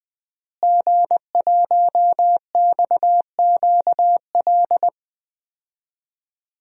Morse code, heard as a keyed tone, about 20 wpm, 700 Hz. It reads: G1XQL
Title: DXCC-100 Call Signs - 5 Characters - Repeated 3x - 20wpm